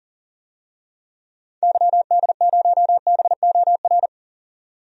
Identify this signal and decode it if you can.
Morse code, heard as a keyed tone, about 40 wpm, 700 Hz. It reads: YD0BOR
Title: DXCC-100 Call Signs - 6 Characters - 40wpm